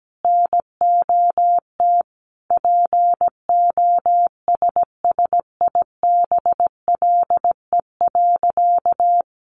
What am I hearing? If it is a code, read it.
Morse code, heard as a keyed tone, about 17 wpm, 700 Hz. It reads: NOT POSSIBLE.
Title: Sentences from Top 600 Words 17wpm